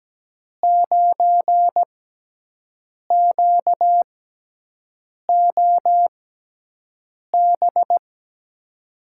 Morse code, heard as a keyed tone, about 17 wpm, spaced as for 6 wpm, 700 Hz. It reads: 9QOB
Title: USA Call Signs - 5 Characters - 6x Character Spacing - 17wpm